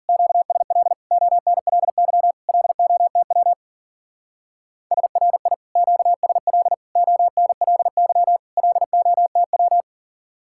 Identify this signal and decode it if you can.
Morse code, heard as a keyed tone, about 48 wpm, 700 Hz. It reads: QSL ONLY LOTW SRI QSL ONLY LOTW
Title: Speed-Racing Common QSO Phrases Encore - 40wpm